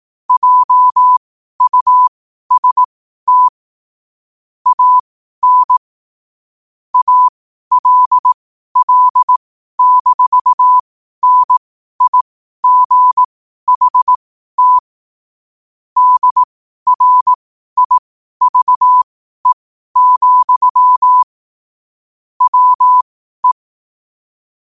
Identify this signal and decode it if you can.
Morse code, heard as a keyed tone, about 18 wpm, spaced as for 12 wpm, 1000 Hz. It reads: JUST AN ALL-NIGHT DRIVE, WE
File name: qotdfiles / 12wpm